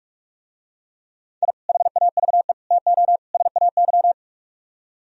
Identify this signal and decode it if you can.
Morse code, heard as a keyed tone, about 44 wpm, 700 Hz. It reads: I HAVE TO SAY